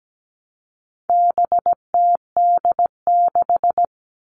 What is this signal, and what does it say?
Morse code, heard as a keyed tone, about 17 wpm, 700 Hz. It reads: BTD6
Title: All Cut Numbers - 4-Digit 17wpm